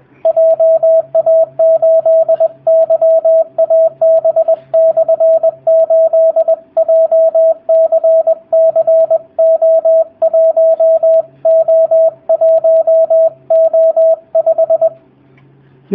Morse code, heard as a keyed tone, about 20 wpm, 650 Hz, with radio noise. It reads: JA8YAB/8 JCCO1O1O5
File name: jh8cbh / 2019cw / sp20- w